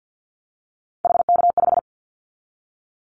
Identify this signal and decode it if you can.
Morse code, heard as a keyed tone, about 50 wpm, 700 Hz. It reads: HK5
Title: Callsign Prefixes for Top 100 DXCC - 50wpm